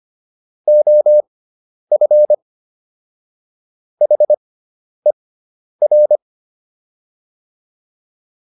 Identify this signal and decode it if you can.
Morse code, heard as a keyed tone, about 25 wpm, spaced as for 10 wpm, 600 Hz. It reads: OF HER